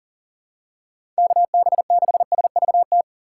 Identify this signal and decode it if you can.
Morse code, heard as a keyed tone, about 40 wpm, 690 Hz. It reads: KB6SVT
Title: USA Call Signs - 6 Characters - 2x Character Spacing - 40wpm